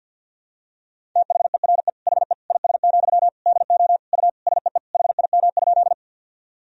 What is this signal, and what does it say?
Morse code, heard as a keyed tone, about 50 wpm, 700 Hz. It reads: THERE HE IS, DO U SEE HIM?
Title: Sentences from Top 100 Words and Abbreviations 50wpm